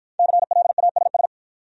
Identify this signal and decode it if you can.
Morse code, heard as a keyed tone, about 53 wpm, 700 Hz. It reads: CLASS